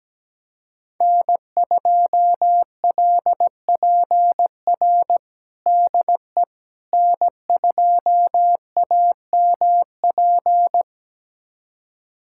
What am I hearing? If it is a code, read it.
Morse code, heard as a keyed tone, about 17 wpm, 700 Hz. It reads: N2LPR DE N2AMP
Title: USA Callsign DE Callsign - 5 Characters - 17wpm